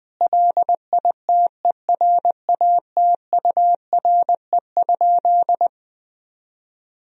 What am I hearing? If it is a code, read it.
Morse code, heard as a keyed tone, about 20 wpm, 700 Hz. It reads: LITERATURE?
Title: Sentences from Top 1000 Words - Review Part 3 - 20wpm